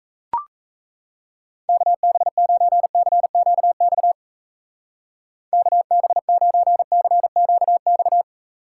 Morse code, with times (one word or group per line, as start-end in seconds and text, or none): 1.7-4.1 KB9CQX
5.5-8.2 KB9CQX